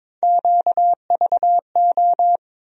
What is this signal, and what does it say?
Morse code, heard as a keyed tone, about 21 wpm, 700 Hz. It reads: QVO
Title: USA Call Sign Postfixes - 3 Letters 22wpm